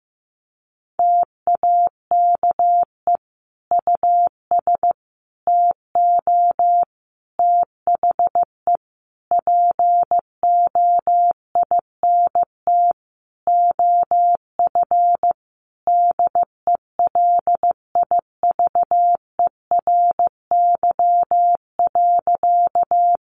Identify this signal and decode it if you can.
Morse code, heard as a keyed tone, about 15 wpm, 700 Hz. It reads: TAKE US TO THE POINT OF DELIVERY.
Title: Sentences from Top 1500 Words 15wpm